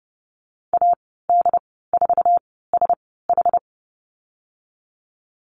Morse code, heard as a keyed tone, about 30 wpm, spaced as for 17 wpm, 700 Hz. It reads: AD4SH
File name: Lesson 19g - Letter w - Call Signs 30wpm